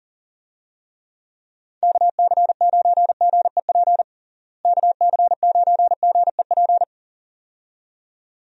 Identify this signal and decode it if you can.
Morse code, heard as a keyed tone, about 40 wpm, 700 Hz. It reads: KC9GEP KC9GEP
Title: USA Call Signs - 6 Characters - Repeated 2x - 40wpm